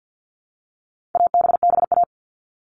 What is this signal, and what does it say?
Morse code, heard as a keyed tone, about 50 wpm, 700 Hz. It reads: ABBA
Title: All Cut Numbers - 4-Digit 50wpm